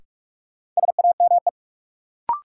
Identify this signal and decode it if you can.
Morse code, heard as a keyed tone, about 46 wpm, 700 Hz. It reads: SAME
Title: Sentences from Top 200 Words - New Word Drill - 45wpm